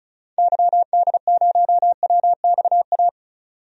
Morse code, heard as a keyed tone, about 36 wpm, 700 Hz. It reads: YD0WXA
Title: DXCC-100 Call Signs - 6 Characters - 35wpm